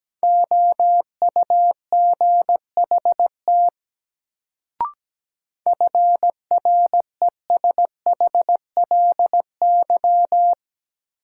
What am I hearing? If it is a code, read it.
Morse code, heard as a keyed tone, about 17 wpm, 700 Hz. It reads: OUGHT FRESHLY
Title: Sets of 5 Words Encore - New Word Drill - 17wpm